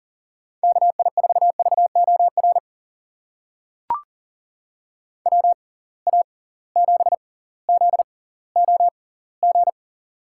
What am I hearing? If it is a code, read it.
Morse code, heard as a keyed tone, about 37 wpm, spaced as for 20 wpm, 700 Hz. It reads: KI4VOR WA7ZOG
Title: USA Call Signs - 6 Characters - 6x Character Spacing - 40wpm